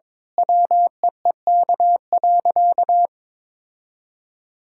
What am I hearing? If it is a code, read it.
Morse code, heard as a keyed tone, about 22 wpm, 700 Hz. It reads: WEEK.
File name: Sentences with 7-Word Phrases - 22wpm